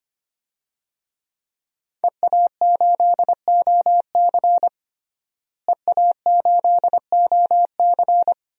Morse code, heard as a keyed tone, about 25 wpm, 700 Hz. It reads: EA8OC EA8OC
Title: DXCC-100 Call Signs - 5 Characters - Repeated 3x - 25wpm